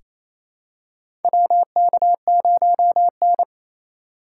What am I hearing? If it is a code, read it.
Morse code, heard as a keyed tone, about 28 wpm, 700 Hz. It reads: WK0N
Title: USA Call Signs - 4 Characters - 3x Character Spacing - 28wpm